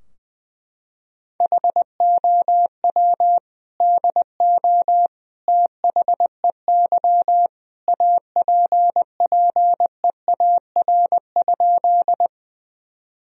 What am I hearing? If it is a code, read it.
Morse code, heard as a keyed tone, about 20 wpm, 700 Hz. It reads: HOW DO THEY APPEAR?
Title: Sentences from Top 1000 Words - Review Part 2 - 20wpm